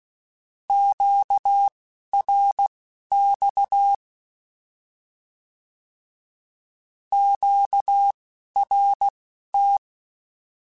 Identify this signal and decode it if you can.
Morse code, heard as a keyed tone, about 16 wpm, spaced as for 11 wpm, 800 Hz. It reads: QRX QRT
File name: Short Runs / Short Runs-mp3s / Words @10